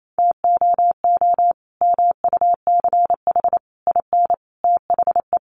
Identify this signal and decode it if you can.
Morse code, heard as a keyed tone, about 28 wpm, 700 Hz. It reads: TOO MUCH IN THE
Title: Sentences from Top 200 Words 28wpm